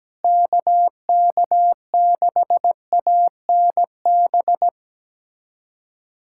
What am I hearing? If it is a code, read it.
Morse code, heard as a keyed tone, about 17 wpm, 700 Hz. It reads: KK6ANB
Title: USA Callsign DE Callsign 17wpm